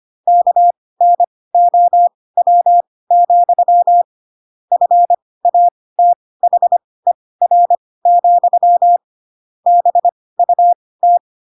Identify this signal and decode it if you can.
Morse code, heard as a keyed tone, about 25 wpm, spaced as for 18 wpm, 700 Hz. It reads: KNOW, FATHER, BUT
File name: Pinocchio / Pinocchio ch7 12 25 ik1hge